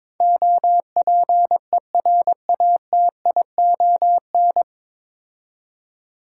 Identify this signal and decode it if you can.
Morse code, heard as a keyed tone, about 22 wpm, 700 Hz. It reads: OPERATION
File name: 9 Letter Words - First 3 Letters Given - 22wpm